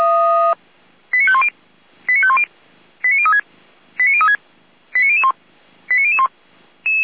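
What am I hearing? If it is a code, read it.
Five-tone selective call (ZVEI), heard as quick runs of five tones, tones of about 70 ms. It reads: 89510 89510 89036 89036 89001 89001
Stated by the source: ZVEI five-tone selective call 89010, received by SDR, from the audio.f85360.feuerwehr-orlamuende.org fire brigade server.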